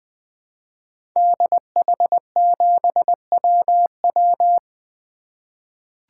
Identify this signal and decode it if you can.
Morse code, heard as a keyed tone, about 20 wpm, 700 Hz. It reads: DH7WW
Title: 2023 SOTA Activator Call Signs - 20wpm